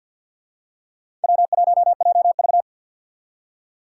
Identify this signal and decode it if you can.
Morse code, heard as a keyed tone, about 53 wpm, 700 Hz. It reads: W1JV
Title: USA Call Signs - 4 Characters - 2x Character Spacing - 50wpm